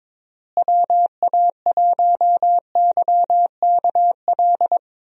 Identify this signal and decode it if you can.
Morse code, heard as a keyed tone, about 22 wpm, 700 Hz. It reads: WA1YKL
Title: USA Call Signs - 6 Characters - Repeated 2x - 22wpm